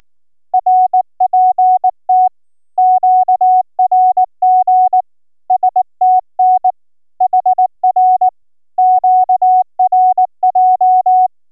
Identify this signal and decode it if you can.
Morse code, heard as a keyed tone, about 17 wpm, 740 Hz. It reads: RPT QRG STN HR QRJ